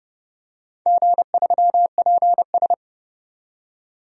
Morse code, heard as a keyed tone, about 30 wpm, 700 Hz. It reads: G3PS
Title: DXCC-100 Call Signs - 4 Characters - 30wpm